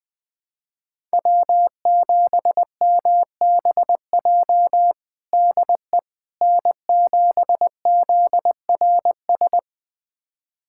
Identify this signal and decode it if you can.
Morse code, heard as a keyed tone, about 21 wpm, 700 Hz. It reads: W7MBJ DE N7ZRS
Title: USA Callsign DE Callsign - 5 Characters - 20wpm